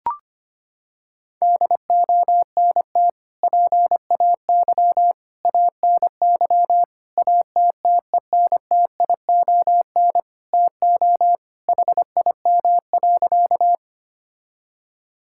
Morse code, heard as a keyed tone, about 25 wpm, 700 Hz. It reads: DONT PAY ANY ATTENTION TO HIM.
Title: Sentences from Top 1000 Words - Review Part 6 - 25wpm